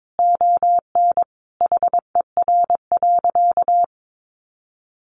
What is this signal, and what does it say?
Morse code, heard as a keyed tone, about 22 wpm, 700 Hz. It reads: ON HER.
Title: Sentences from Top 1200 Words 22wpm